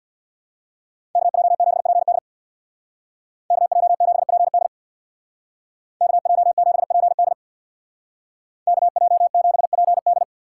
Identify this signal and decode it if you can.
Morse code, heard as a keyed tone, about 64 wpm, 700 Hz. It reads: KJ7PD KJ7PD KJ7PD KJ7PD